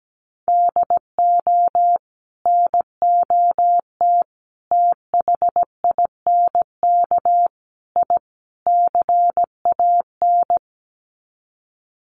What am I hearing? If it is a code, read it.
Morse code, heard as a keyed tone, about 17 wpm, 700 Hz. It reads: DO NOT THINK I CAN
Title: Sets of 5 Words Encore2 17wpm